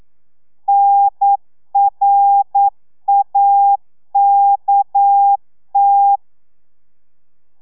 Morse code, heard as a keyed tone, about 9 wpm, 800 Hz. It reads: NRAKT
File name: telegrafia / i2tfj / Lez6 40